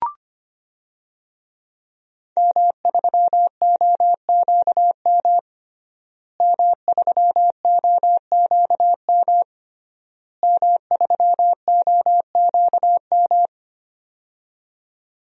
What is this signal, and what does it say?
Morse code, heard as a keyed tone, about 25 wpm, 700 Hz. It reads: M3OQM M3OQM M3OQM
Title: DXCC-100 Call Signs - 5 Characters - Repeated 3x - 25wpm